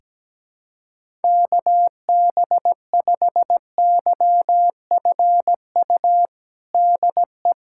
Morse code, heard as a keyed tone, about 17 wpm, 700 Hz. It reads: KB5YFU DE
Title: USA Callsign DE Callsign 17wpm